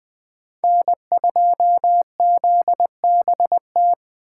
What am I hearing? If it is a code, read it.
Morse code, heard as a keyed tone, about 20 wpm, 700 Hz. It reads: N2ZBT
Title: Speed Racing - USA 1x3 Call Signs 17wpm